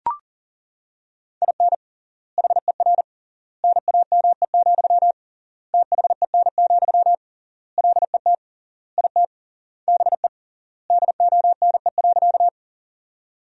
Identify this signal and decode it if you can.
Morse code, heard as a keyed tone, about 40 wpm, 700 Hz. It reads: IN HER NAME, THEN, LET IT BE DONE.
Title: 3x Word-Spacing - Sentences from Top 500 Words - 40wpm